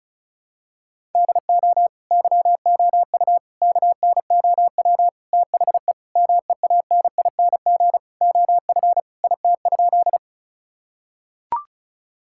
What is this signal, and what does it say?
Morse code, heard as a keyed tone, about 36 wpm, 700 Hz. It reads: DO YOU KNOW THE MEANING OF IT?
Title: Sentences from Top 1000 Words - Review Part 6 - 35wpm